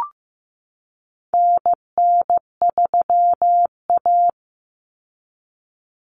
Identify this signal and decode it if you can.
Morse code, heard as a keyed tone, about 15 wpm, 700 Hz. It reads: NN3A
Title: Common Cut Numbers - 4-Digit 15wpm